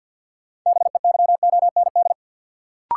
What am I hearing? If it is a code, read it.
Morse code, heard as a keyed tone, about 52 wpm, 700 Hz. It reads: BEYOND
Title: Sentences from Top 1100 Words - New Word Drill Alt - 50wpm